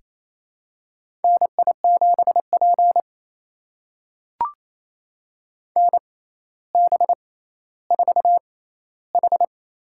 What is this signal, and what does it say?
Morse code, heard as a keyed tone, about 27 wpm, spaced as for 14 wpm, 700 Hz. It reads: NI7P NB4H
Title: USA Call Signs - 4 Characters - 6x Character Spacing - 28wpm